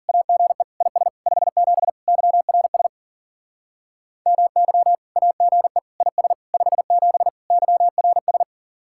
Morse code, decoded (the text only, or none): AGE IS 57 YRS MY AGE IS 57 YRS